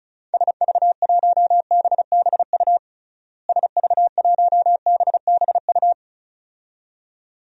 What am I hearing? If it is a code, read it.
Morse code, heard as a keyed tone, about 34 wpm, 700 Hz. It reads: SV1BBU SV1BBU
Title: DXCC-100 Call Signs - 6 Characters - Repeated 2x - 35wpm